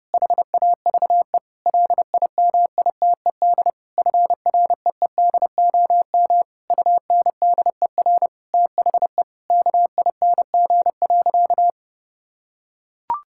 Morse code, heard as a keyed tone, about 30 wpm, 700 Hz. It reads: HAVE LIMITED FREEDOM UNDER THE KING.
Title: Sentences from Top 1000 Words - Review Part 6 - 30wpm